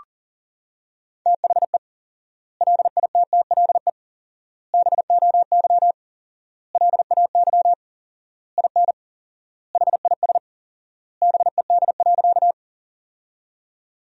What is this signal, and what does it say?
Morse code, heard as a keyed tone, about 40 wpm, 700 Hz. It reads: THE LITTLE BOY LAY IN HIS BED.
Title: 4x Word-Spacing - Sentences from Top 500 Words - 40wpm